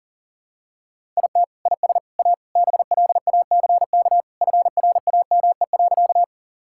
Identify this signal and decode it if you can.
Morse code, heard as a keyed tone, about 40 wpm, 700 Hz. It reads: IT IS A BLACK FRAME.
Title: Sentences from Top 1000 Words - Review Part 6 - 40wpm